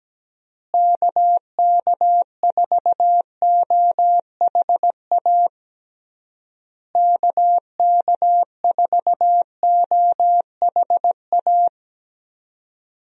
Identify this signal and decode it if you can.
Morse code, heard as a keyed tone, about 17 wpm, 700 Hz. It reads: KK4OHA KK4OHA
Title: USA Call Signs - 6 Characters - Repeated 3x - 17wpm